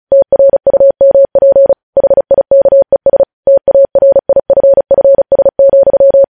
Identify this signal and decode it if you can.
Morse code, heard as a keyed tone, about 36 wpm, 550 Hz. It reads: TRUMP HIKES TARIFFS,
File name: podcast / News headlines-1x-35x35